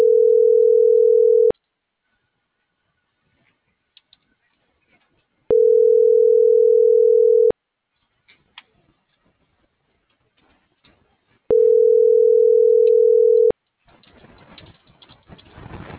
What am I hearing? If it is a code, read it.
Ringback tone (North America), 440 + 480 Hz.